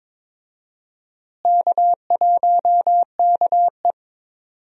Morse code, heard as a keyed tone, about 22 wpm, 700 Hz. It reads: K1KE
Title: USA Call Signs - 4 Characters - Repeated 3x - 22wpm